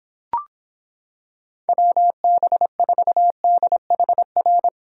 Morse code, heard as a keyed tone, about 26 wpm, 700 Hz. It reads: WB4DHR